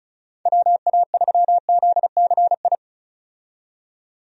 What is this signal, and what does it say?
Morse code, heard as a keyed tone, about 34 wpm, 700 Hz. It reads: WA3ZCI
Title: USA Call Signs - 6 Characters - 35wpm